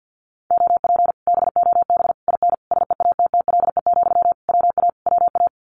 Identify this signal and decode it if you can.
Morse code, heard as a keyed tone, about 50 wpm, 700 Hz. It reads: OP BOB IN SEATTLE, WA WA
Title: Common QSO Phrases Encore 50wpm